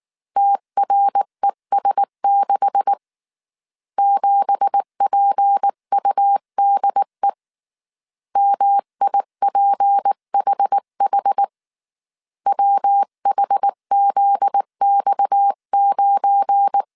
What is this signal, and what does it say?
Morse code, heard as a keyed tone, about 19 wpm, 800 Hz. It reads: TRES6 7PUDE MIPHH WHZX9
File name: CW / DET / 1 9 1500C S B 18wpm 2551